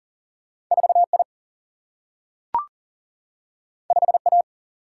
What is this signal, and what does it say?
Morse code, heard as a keyed tone, about 40 wpm, 700 Hz. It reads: 4I 5A
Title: Single-Letter Single-Number 40wpm